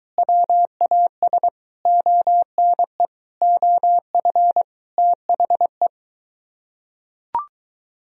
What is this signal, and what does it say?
Morse code, heard as a keyed tone, about 23 wpm, 700 Hz. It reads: WAS ONE OF THE